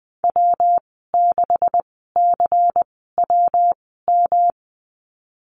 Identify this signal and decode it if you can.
Morse code, heard as a keyed tone, about 20 wpm, spaced as for 15 wpm, 700 Hz. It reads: W6CWM
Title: 128 Lesson 21g - Number 6 - Call Signs 20wpm